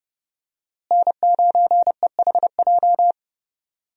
Morse code, heard as a keyed tone, about 30 wpm, 700 Hz. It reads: N9EHJ
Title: USA Call Signs - 5 Characters - Repeated 2x - 30wpm